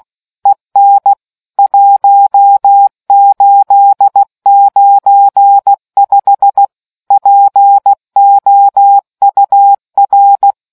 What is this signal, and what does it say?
Morse code, heard as a keyed tone, about 16 wpm, 800 Hz. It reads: EN 1895 POUR